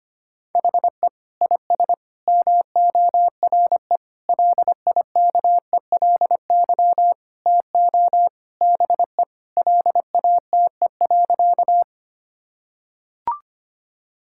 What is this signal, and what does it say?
Morse code, heard as a keyed tone, about 25 wpm, 700 Hz. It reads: HE IS MORE LIKELY TO BE LATE.
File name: practice / 25wpm / Sentences with 4-Word Phrases - 25wpm